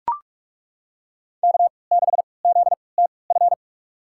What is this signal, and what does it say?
Morse code, heard as a keyed tone, about 42 wpm, spaced as for 25 wpm, 700 Hz. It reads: K6ZTF
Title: USA Call Signs - 5 Characters - 3x Character Spacing - 45wpm